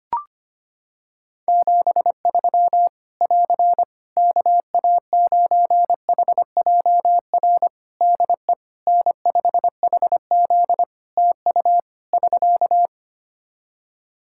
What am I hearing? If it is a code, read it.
Morse code, heard as a keyed tone, about 25 wpm, 700 Hz. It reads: 73 <AR> KA9HJR DE N5HZ TU <SK>
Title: Practice QSO - Exchange 4 - Ending - 25wpm